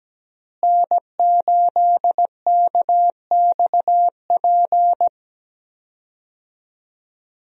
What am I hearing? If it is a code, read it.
Morse code, heard as a keyed tone, about 17 wpm, 700 Hz. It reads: N8KXP
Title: USA Call Signs - 5 Characters - Repeated 3x - 17wpm